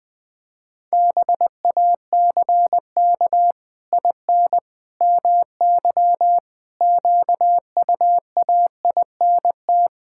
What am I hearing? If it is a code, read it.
Morse code, heard as a keyed tone, about 21 wpm, 700 Hz. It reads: BACK IN MY QUAINT
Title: Pangram Challenge v2 - 20wpm